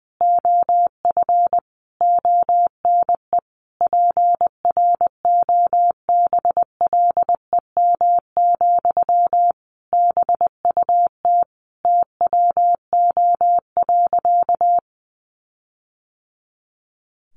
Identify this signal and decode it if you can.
Morse code, heard as a keyed tone, about 20 wpm, 700 Hz. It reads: OF ONE PROBLEM, BUT TWO.
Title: Sentences from Top 1400 Words 20wpm